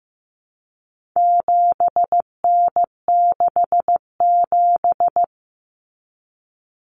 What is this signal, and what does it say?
Morse code, heard as a keyed tone, about 15 wpm, 700 Hz. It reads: 7N67
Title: Common Cut Numbers - 4-Digit 15wpm